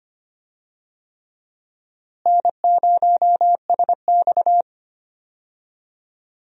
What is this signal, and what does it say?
Morse code, heard as a keyed tone, about 25 wpm, 700 Hz. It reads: N0SX
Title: USA Call Signs - 4 Characters - Repeated 3x - 25wpm